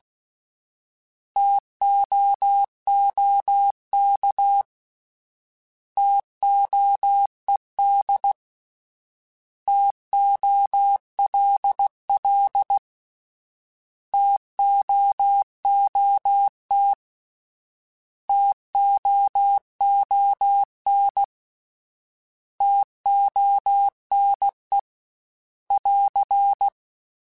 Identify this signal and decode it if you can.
Morse code, heard as a keyed tone, about 16 wpm, 800 Hz. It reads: TOOK TOED TOLL TOOT TOON TONE <AR>